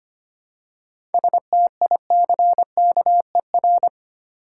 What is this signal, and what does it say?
Morse code, heard as a keyed tone, about 25 wpm, 690 Hz. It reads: STICKER
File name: practice / 25wpm / Easy Words with Hints - 25wpm